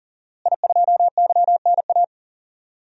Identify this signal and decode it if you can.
Morse code, heard as a keyed tone, about 37 wpm, 700 Hz. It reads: I2YNA
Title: DXCC-100 Call Signs - 5 Characters - Repeated 2x - 40wpm